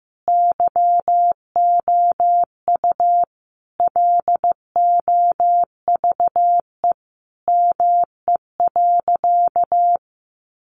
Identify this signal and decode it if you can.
Morse code, heard as a keyed tone, about 15 wpm, 700 Hz. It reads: YOU LOVE ME.
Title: Sentences from Top 600 Words 15wpm